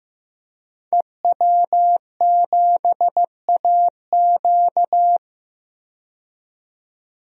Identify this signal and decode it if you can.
Morse code, heard as a keyed tone, about 15 wpm, 700 Hz. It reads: EW7AQ
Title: DXCC-100 Call Signs - 5 Characters - 15wpm